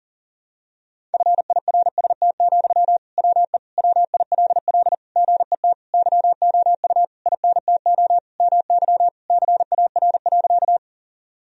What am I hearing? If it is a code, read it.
Morse code, heard as a keyed tone, about 40 wpm, 700 Hz. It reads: FIRST, WE WILL GET YOU INTO MY CAR.
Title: Sentences from Top 1000 Words - Review Part 1 - 40wpm